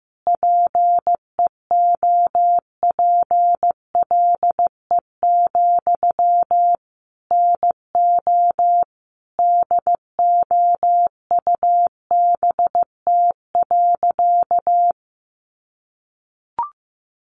Morse code, heard as a keyed tone, about 15 wpm, 700 Hz. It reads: PEOPLE, NO DOUBT.
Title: Sentences from Top 1500 Words 15wpm